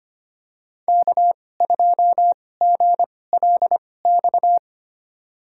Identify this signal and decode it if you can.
Morse code, heard as a keyed tone, about 25 wpm, spaced as for 18 wpm, 700 Hz. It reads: K2GLX
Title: USA Call Signs - 5 Characters - 2x Character Spacing - 25wpm